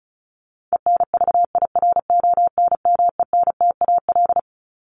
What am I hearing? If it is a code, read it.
Morse code, heard as a keyed tone, about 34 wpm, 700 Hz. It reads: ENVIRONMENTAL
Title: Top 1001-1250 Words 35wpm